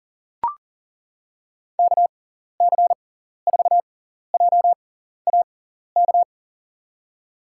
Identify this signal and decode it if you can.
Morse code, heard as a keyed tone, about 40 wpm, spaced as for 14 wpm, 700 Hz. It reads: KC4JAK
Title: USA Call Signs - 6x Character Spacing - 40wpm